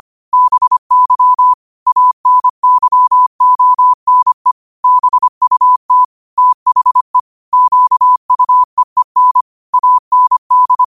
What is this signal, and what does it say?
Morse code, heard as a keyed tone, about 25 wpm, 1000 Hz. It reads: BY ANYONE BUT THE QUEEN AND
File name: qotdfiles / 25wpm